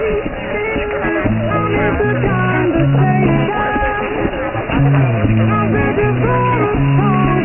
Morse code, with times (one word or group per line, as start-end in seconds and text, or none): none